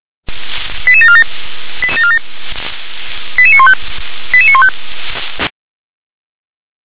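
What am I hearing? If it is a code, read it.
Five-tone selective call (ZVEI), heard as quick runs of five tones, tones of about 70 ms. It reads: 88747 88915 88915
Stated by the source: Received on an SDR.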